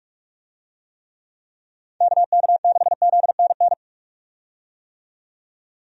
Morse code, heard as a keyed tone, about 42 wpm, 700 Hz. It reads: KK6ZNN
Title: USA Call Signs - 6 Characters - 45wpm